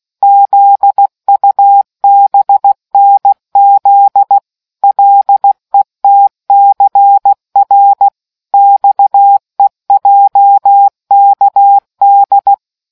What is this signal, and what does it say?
Morse code, heard as a keyed tone, about 16 wpm, 800 Hz. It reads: ZUBNZ LETCR XEJKD